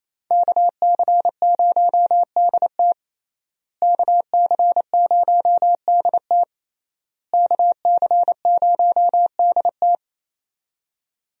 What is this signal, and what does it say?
Morse code, heard as a keyed tone, about 28 wpm, 700 Hz. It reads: KC0DT KC0DT KC0DT
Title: USA Call Signs - 5 Characters - Repeated 3x - 28wpm